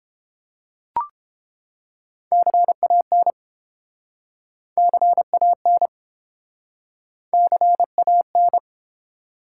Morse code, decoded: CAN CAN CAN

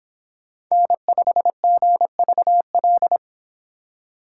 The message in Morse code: N5GVL

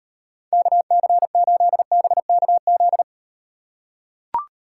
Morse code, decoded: KC8BKZ

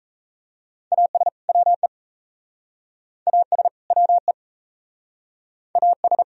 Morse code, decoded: AS WE AS WE AS

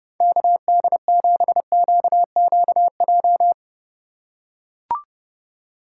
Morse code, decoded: KD7QQJ

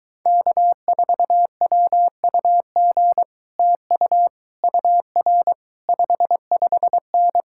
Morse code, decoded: K4WUG TU UR 55N

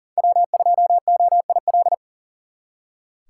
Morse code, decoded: W2OIL

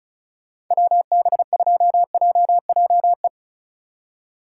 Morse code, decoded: WB2JJE

